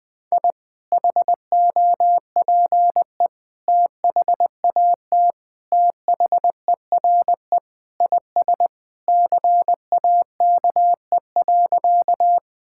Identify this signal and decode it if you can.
Morse code, heard as a keyed tone, about 21 wpm, 700 Hz. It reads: I HOPE THAT THERE IS CAKE.
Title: Sentences with 2-Word Phrases - 20wpm